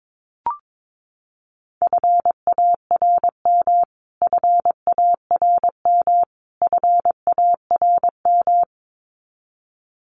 Morse code, heard as a keyed tone, about 22 wpm, 700 Hz. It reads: FARM FARM FARM
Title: Top 500 Words Repeated 22wpm